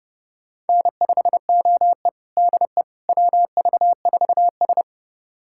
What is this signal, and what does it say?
Morse code, heard as a keyed tone, about 30 wpm, 700 Hz. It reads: N5OE DE WV4S
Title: USA Callsign DE Callsign - 4 Characters - 30wpm